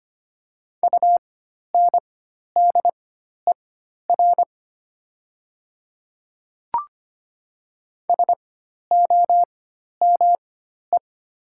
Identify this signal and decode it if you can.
Morse code, heard as a keyed tone, about 25 wpm, spaced as for 12 wpm, 700 Hz. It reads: UNDER SOME
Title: 4x Character Spacing - Top 300 Words - 25wpm